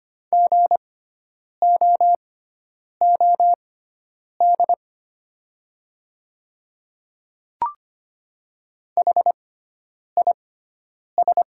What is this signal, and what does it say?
Morse code, heard as a keyed tone, about 25 wpm, spaced as for 9 wpm, 700 Hz. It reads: GOOD HIS
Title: 6x Character Spacing - Top 200 Words - 25wpm